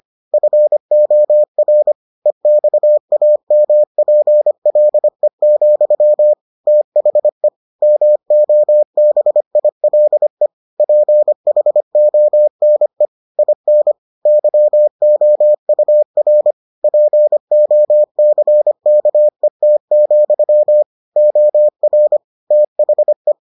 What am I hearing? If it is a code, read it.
Morse code, heard as a keyed tone, about 25 wpm, 600 Hz. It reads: FOR EXAMPLE, THE MOBILE PHONE IN YOUR POCKET, OR THE